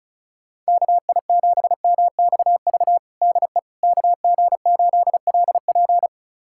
Morse code, heard as a keyed tone, about 36 wpm, 700 Hz. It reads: KI7MXV DE KG8LP
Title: USA Callsign DE Callsign 35wpm